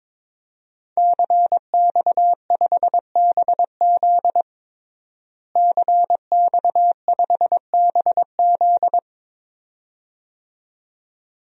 Morse code, heard as a keyed tone, about 22 wpm, 700 Hz. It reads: CX5BZ CX5BZ
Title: DXCC-100 Call Signs - 5 Characters - Repeated 2x - 22wpm